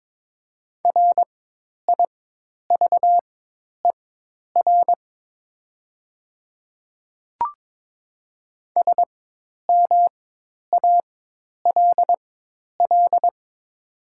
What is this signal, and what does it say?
Morse code, heard as a keyed tone, about 22 wpm, spaced as for 10 wpm, 700 Hz. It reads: RIVER SMALL
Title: 4x Character Spacing - Top 500 Words - 22wpm